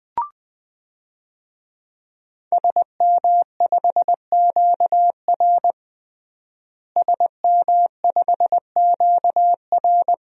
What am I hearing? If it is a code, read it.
Morse code, heard as a keyed tone, about 20 wpm, 700 Hz. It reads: SM5QR SM5QR
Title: DXCC-100 Call Signs - 5 Characters - Repeated 3x - 20wpm